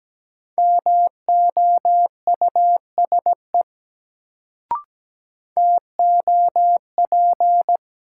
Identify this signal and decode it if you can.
Morse code, heard as a keyed tone, about 17 wpm, 700 Hz. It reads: MOUSE TOP
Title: Sentences from Top 700 Words - New Word Drill - 17wpm